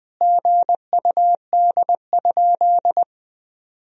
Morse code, heard as a keyed tone, about 20 wpm, 700 Hz. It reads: GUD?
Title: Sentences from Top 200 Words and Abbreviations 20wpm